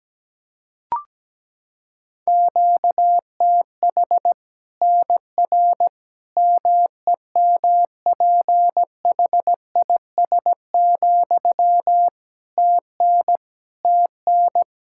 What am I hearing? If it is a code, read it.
Morse code, heard as a keyed tone, about 17 wpm, 700 Hz. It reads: QTH NR MEMPHIS, TN TN
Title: Practice QSO - Exchange 1 - Beginning - 17wpm